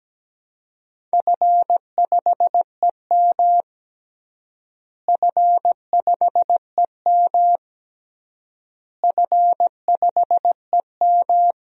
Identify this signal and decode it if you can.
Morse code, heard as a keyed tone, about 17 wpm, 700 Hz. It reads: F5EM F5EM F5EM
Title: DXCC-100 Call Signs - 4 Characters - Repeated 3x - 17wpm